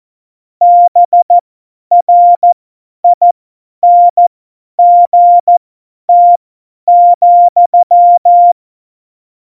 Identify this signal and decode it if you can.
Morse code, heard as a keyed tone, about 14 wpm, spaced as for 10 wpm, 700 Hz. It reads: BRINGT,
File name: Morsekurs / Witze / Witz 26. Januar 2021, 50 F70